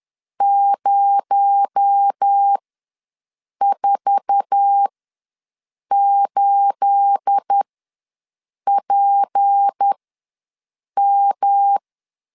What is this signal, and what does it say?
Morse code, heard as a keyed tone, about 10 wpm, spaced as for 6 wpm, 800 Hz. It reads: 048PM
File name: LEZIONI / 9 28 ws 10wpm 4124